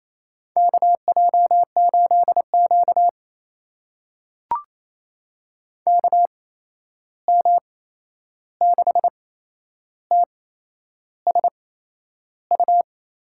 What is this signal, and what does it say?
Morse code, heard as a keyed tone, about 28 wpm, spaced as for 11 wpm, 700 Hz. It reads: KJ8Q KM6TSU
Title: USA Call Signs - 8x Character Spacing - 28wpm